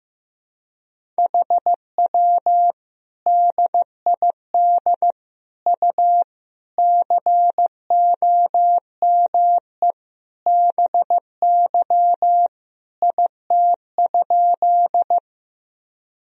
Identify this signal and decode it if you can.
Morse code, heard as a keyed tone, about 15 wpm, 700 Hz. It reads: HW DID U COME BY IT?
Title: Sentences from Top 100 Words and Abbreviations 15wpm